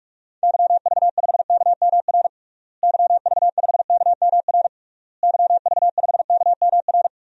Morse code, heard as a keyed tone, about 46 wpm, 700 Hz. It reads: YV5KMR YV5KMR YV5KMR
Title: DXCC-100 Call Signs - 6 Characters - Repeated 3x - 45wpm